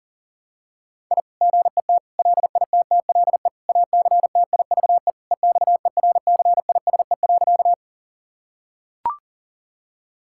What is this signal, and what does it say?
Morse code, heard as a keyed tone, about 40 wpm, 700 Hz. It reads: I GET LITTLE ACTIVE EXERCISE.
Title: Sentences from Top 1000 Words - Review Part 6 - 40wpm